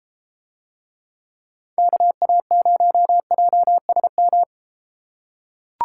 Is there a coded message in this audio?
Yes: Morse code from 1.8 to 4.4 s, heard as a keyed tone, about 30 wpm, 700 Hz. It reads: KA0JSM